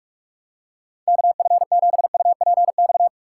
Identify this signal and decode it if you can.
Morse code, heard as a keyed tone, about 47 wpm, 700 Hz. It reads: KF7UPX
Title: USA Call Signs - 6 Characters - 2x Character Spacing - 45wpm